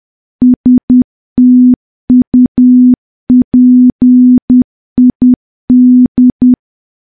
Morse code, heard as a keyed tone, about 10 wpm, 250 Hz. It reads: STUPID
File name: morse / morse-training-10WPM